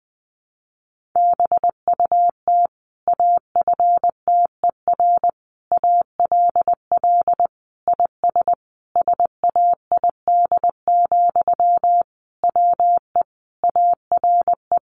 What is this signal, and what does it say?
Morse code, heard as a keyed tone, about 20 wpm, 700 Hz. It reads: BUT AFTER ALL IS SAID, WE ARE
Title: Sentences from Top 1500 Words 20wpm